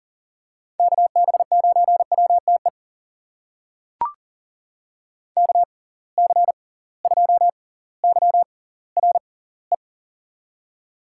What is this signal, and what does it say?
Morse code, heard as a keyed tone, about 40 wpm, spaced as for 20 wpm, 700 Hz. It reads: KB9WTE KC2YRE
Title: USA Call Signs - 6 Characters - 6x Character Spacing - 40wpm